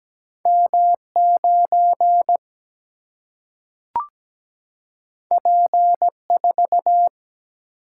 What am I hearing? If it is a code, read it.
Morse code, heard as a keyed tone, about 17 wpm, 700 Hz. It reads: M 9 P 4
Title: Single-Letter Single-Number 17wpm